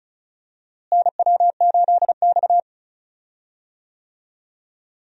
Morse code, heard as a keyed tone, about 34 wpm, 700 Hz. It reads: NW8X
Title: USA Call Signs - 4 Characters - Repeated 3x - 35wpm